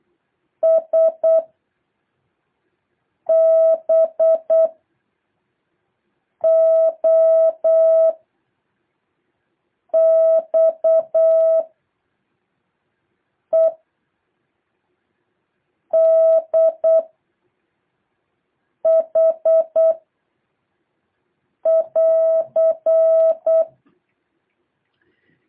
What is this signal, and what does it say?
Morse code, heard as a keyed tone, about 8 wpm, 660 Hz. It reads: S B O X E D H <AR>
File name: jh8cbh / 2023cw